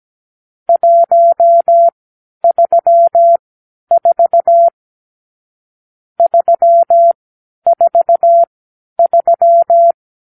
Morse code, heard as a keyed tone, about 17 wpm, spaced as for 10 wpm, 680 Hz. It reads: 134 343
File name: k7qo / Lesson 46 (1-4)